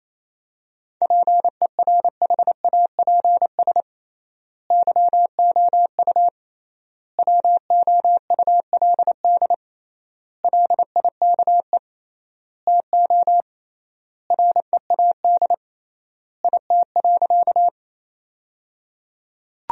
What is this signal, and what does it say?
Morse code, heard as a keyed tone, about 27 wpm, 700 Hz. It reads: PERHAPS YOU WOULD LIKE TO READ IT.
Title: 3x Word-Spacing - Sentences from Top 500 Words - 28wpm